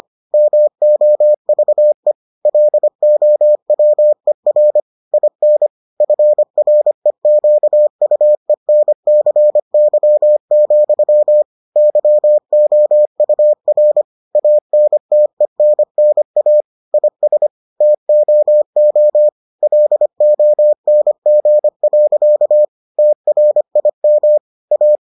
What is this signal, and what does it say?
Morse code, heard as a keyed tone, about 25 wpm, 600 Hz. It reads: MOVE LOWER IN FREQUENCY, YOUR ANTENNA IS TOO LONG. TRIM A